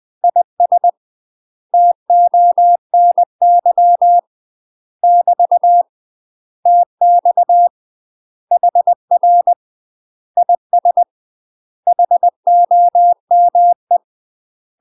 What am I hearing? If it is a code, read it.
Morse code, heard as a keyed tone, about 20 wpm, 700 Hz. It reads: IS TONY <BT> TX HR IS HOME
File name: UK MASTER QSOs TRAINING / 20WPM / UK MASTER QSOs 0048